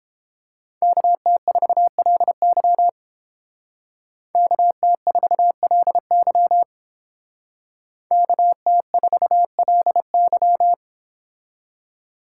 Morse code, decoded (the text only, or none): KT4LY KT4LY KT4LY